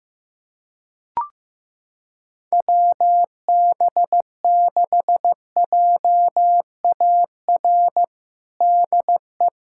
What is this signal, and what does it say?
Morse code, heard as a keyed tone, about 15 wpm, 700 Hz. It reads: WB6JAR DE
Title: USA Callsign DE Callsign - 6 Characters - 15wpm